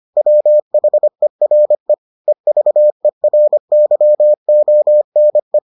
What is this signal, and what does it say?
Morse code, heard as a keyed tone, about 25 wpm, 600 Hz. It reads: WHERE EVERYONE